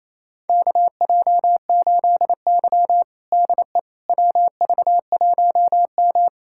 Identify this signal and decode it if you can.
Morse code, heard as a keyed tone, about 28 wpm, 700 Hz. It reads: KJ8Y DE WV1M
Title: USA Callsign DE Callsign - 4 Characters - 28wpm